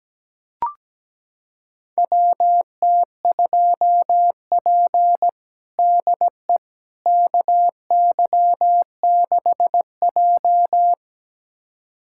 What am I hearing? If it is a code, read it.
Morse code, heard as a keyed tone, about 17 wpm, 700 Hz. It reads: WT2P DE KY6J
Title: USA Callsign DE Callsign - 4 Characters - 17wpm